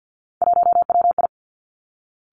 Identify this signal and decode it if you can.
Morse code, heard as a keyed tone, about 50 wpm, 700 Hz. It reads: 1WI
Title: Three Challenging Letter-Number Combinations 50wpm